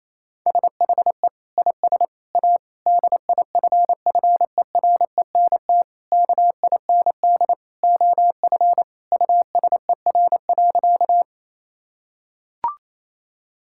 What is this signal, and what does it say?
Morse code, heard as a keyed tone, about 28 wpm, 700 Hz. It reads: SHE IS A DIFFERENT KIND OF USER.
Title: Sentences from Top 1000 Words - Review Part 5 - 28wpm